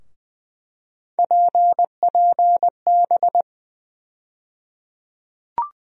Morse code, heard as a keyed tone, about 20 wpm, 700 Hz. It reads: PPB
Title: USA Call Sign Postfixes - 3 Letters 20wpm